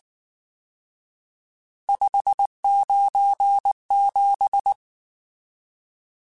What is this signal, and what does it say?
Morse code, heard as a keyed tone, about 19 wpm, 800 Hz. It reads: 597
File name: RSTs / RST-mp3s / Various RST @18